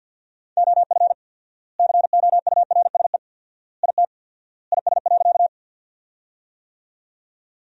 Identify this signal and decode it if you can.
Morse code, heard as a keyed tone, about 50 wpm, 700 Hz. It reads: OF COURSE IT IS.
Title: 4x Word-Spacing - Sentences from Top 500 Words - 50wpm